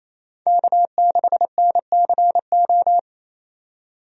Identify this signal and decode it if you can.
Morse code, heard as a keyed tone, about 28 wpm, 700 Hz. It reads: K6NCO
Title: USA Call Signs - 5 Characters - 28wpm